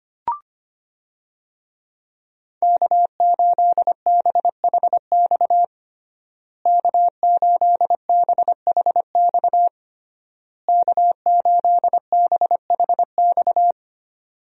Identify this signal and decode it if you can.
Morse code, heard as a keyed tone, about 25 wpm, 700 Hz. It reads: K8BHX K8BHX K8BHX
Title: USA Call Signs - 5 Characters - Repeated 3x - 25wpm